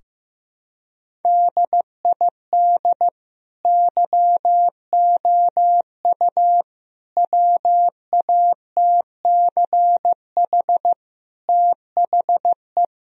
Morse code, heard as a keyed tone, about 15 wpm, 700 Hz. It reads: DID YOU WATCH THE
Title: Sentences from Top 1000 Words - Review Part 4 - 15wpm